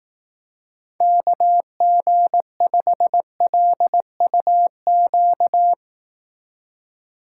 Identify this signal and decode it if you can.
Morse code, heard as a keyed tone, about 18 wpm, 700 Hz. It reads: KG5LUQ
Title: Speed Racing - USA 2x3 Call Signs 15wpm